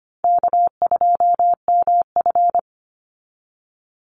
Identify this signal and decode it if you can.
Morse code, heard as a keyed tone, about 25 wpm, 700 Hz. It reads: K2MF
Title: Easy Call Signs Repeated 2x - 25wpm